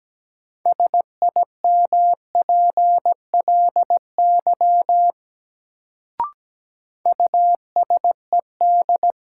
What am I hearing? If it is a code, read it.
Morse code, heard as a keyed tone, about 17 wpm, 700 Hz. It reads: SIMPLY USED
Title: Sentences from Top 600 Words - New Word Drill - 17wpm